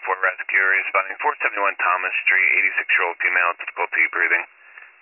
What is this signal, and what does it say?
Emergency Alert System (SAME) tones, heard as harsh buzzing bursts, with speech. It is heard from 0.3 to 4.4 s.